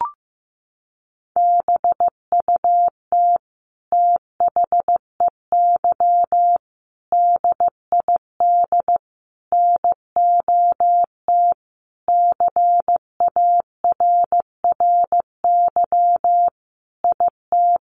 Morse code, heard as a keyed tone, about 14 wpm, 700 Hz. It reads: BUT THEY DID NOT CARRY IT
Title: Sentences from Top 1400 Words 15wpm